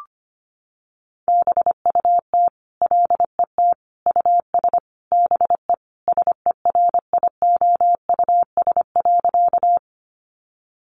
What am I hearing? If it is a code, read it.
Morse code, heard as a keyed tone, about 25 wpm, 700 Hz. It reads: BUT LET US BE SERIOUS.
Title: Sentences from Top 1100 Words 25wpm